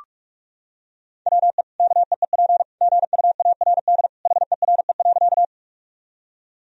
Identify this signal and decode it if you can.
Morse code, heard as a keyed tone, about 46 wpm, 700 Hz. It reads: WE KEEP GUARD HERE.
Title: Sentences from Top 1000 Words - Review Part 4 - 45wpm